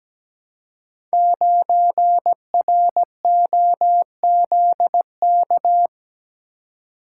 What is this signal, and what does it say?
Morse code, heard as a keyed tone, about 17 wpm, 700 Hz. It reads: 9ROZK
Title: Code Groups - 5 Letter:Numbers - 17wpm